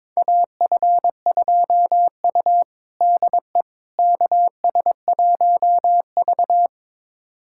Morse code, heard as a keyed tone, about 22 wpm, 700 Hz. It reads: AF2U DE KS1V
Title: USA Callsign DE Callsign - 4 Characters - 22wpm